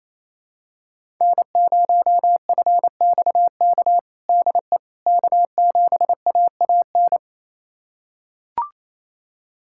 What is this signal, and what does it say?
Morse code, heard as a keyed tone, about 28 wpm, 700 Hz. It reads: N0FXK DE K7AAN
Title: USA Callsign DE Callsign - 5 Characters - 28wpm